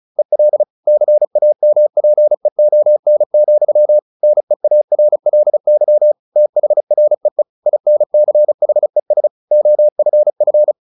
Morse code, heard as a keyed tone, about 34 wpm, 600 Hz. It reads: EL CAMPEON, NEARLY THREE INCHES OFF
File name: the old man and the sea 2 35wpm